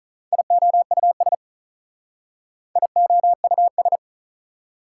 Morse code, heard as a keyed tone, about 43 wpm, 690 Hz. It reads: IOUS IOUS
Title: Speed-Racing Prefixes and Suffixes 35wpm